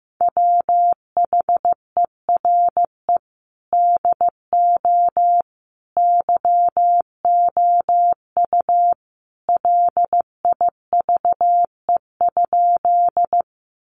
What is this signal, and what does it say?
Morse code, heard as a keyed tone, about 15 wpm, 700 Hz. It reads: WHERE DO YOU LIVE?
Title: Sentences from Top 200 Words 15wpm